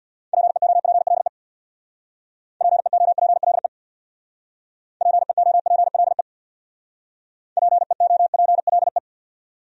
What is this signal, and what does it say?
Morse code, heard as a keyed tone, about 63 wpm, 700 Hz. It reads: PEOPLE PEOPLE PEOPLE PEOPLE